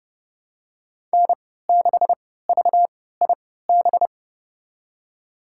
Morse code, heard as a keyed tone, about 32 wpm, spaced as for 17 wpm, 700 Hz. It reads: N6VIB
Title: USA Call Signs - 5 Characters - 3x Character Spacing - 30wpm